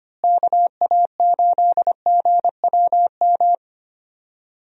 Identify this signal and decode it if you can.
Morse code, heard as a keyed tone, about 25 wpm, 700 Hz. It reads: KA8GWM
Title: USA Call Signs - 6 Characters - Repeated 2x - 25wpm